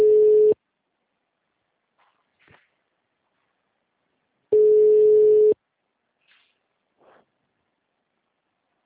Ringback tone (Europe, 425 Hz).